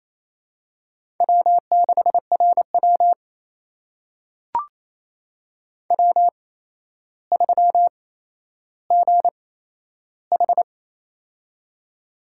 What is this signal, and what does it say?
Morse code, heard as a keyed tone, about 28 wpm, spaced as for 12 wpm, 700 Hz. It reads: W6RW W3GH
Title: USA Call Signs - 4 Characters - 8x Character Spacing - 28wpm